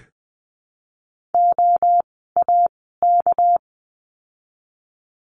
Morse code, heard as a keyed tone, about 20 wpm, spaced as for 15 wpm, 700 Hz. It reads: OAK
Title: 199 Lesson 32f - Letter k - Words 20wpm